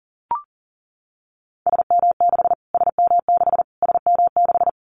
Morse code, heard as a keyed tone, about 40 wpm, 700 Hz. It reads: SM6 SM6 SM6
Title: Callsign Prefixes for Top 25 DXCC - Repeated 3x - 40wpm